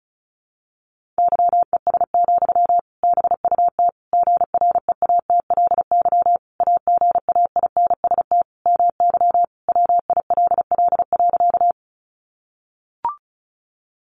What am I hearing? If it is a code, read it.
Morse code, heard as a keyed tone, about 34 wpm, 700 Hz. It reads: YES, BUT GREATLY AGAINST MY WILL.
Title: Sentences from Top 1400 Words 35wpm